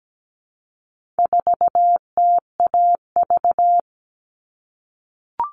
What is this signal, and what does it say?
Morse code, heard as a keyed tone, about 17 wpm, 700 Hz. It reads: 4TAV
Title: All Cut Numbers - 4-Digit 17wpm